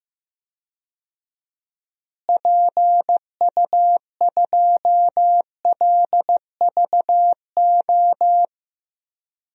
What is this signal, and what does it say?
Morse code, heard as a keyed tone, about 15 wpm, 700 Hz. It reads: PU2LVO
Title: DXCC-100 Call Signs - 6 Characters - Repeated 3x - 15wpm